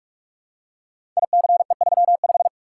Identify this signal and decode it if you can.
Morse code, heard as a keyed tone, about 44 wpm, 700 Hz. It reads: ICE35
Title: Code Groups - 5 Letter:Numbers - 45wpm